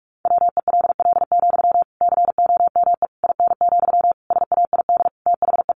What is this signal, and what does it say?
Morse code, heard as a keyed tone, about 44 wpm, 700 Hz. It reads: WELL, COME IN, SAID THE
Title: Sentences from Top 1000 Words 45wpm